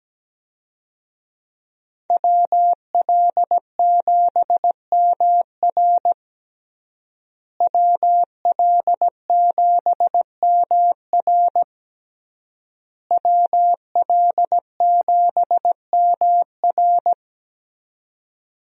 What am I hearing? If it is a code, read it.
Morse code, heard as a keyed tone, about 17 wpm, 700 Hz. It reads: WL7MR WL7MR WL7MR
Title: USA Call Signs - 5 Characters - Repeated 3x - 17wpm